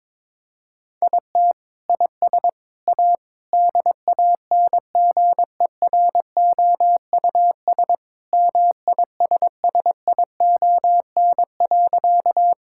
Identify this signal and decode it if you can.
Morse code, heard as a keyed tone, about 22 wpm, 700 Hz. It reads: IT IS A DANGEROUS MISSION.
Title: Sentences from Top 1600 Words 22wpm